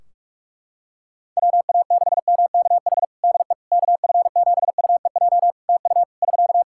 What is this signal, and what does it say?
Morse code, heard as a keyed tone, about 46 wpm, 700 Hz. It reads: WA6MKH DE KF7UEJ TU <SK>
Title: K1USN SST QSO Practice 45wpm